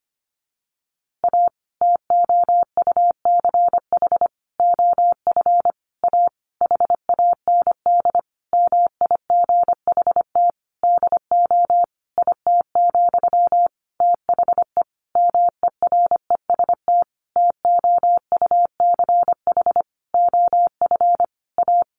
Morse code, heard as a keyed tone, about 25 wpm, 700 Hz. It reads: A TOUCH OF A HAND MIGHT DO IT, THE MEREST TOUCH OF A
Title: A Sound Of Thunder 25wpm